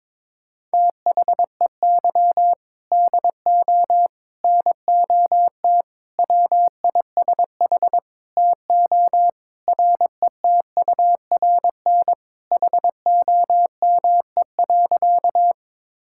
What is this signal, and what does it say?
Morse code, heard as a keyed tone, about 22 wpm, 700 Hz. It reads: THEY DO NOT WISH TO RETURN HOME.
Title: Sentences from Top 1600 Words 22wpm